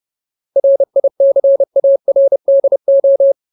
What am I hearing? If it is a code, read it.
Morse code, heard as a keyed tone, about 30 wpm, 550 Hz. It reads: RICARDO
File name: Ultimate Word List / 30WPM Ultimate